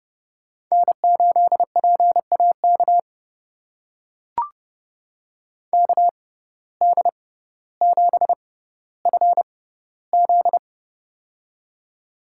Morse code, heard as a keyed tone, about 30 wpm, spaced as for 15 wpm, 700 Hz. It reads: N8PAK KD7FZ
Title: USA Call Signs - 5 Characters - 6x Character Spacing - 30wpm